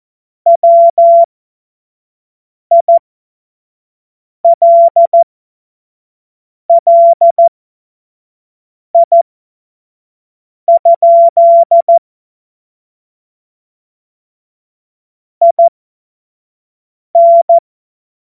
Morse code, heard as a keyed tone, about 14 wpm, spaced as for 5 wpm, 680 Hz. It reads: WILLI? IN
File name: Morsekurs / Witze / Witz 22. November 2022 25 F70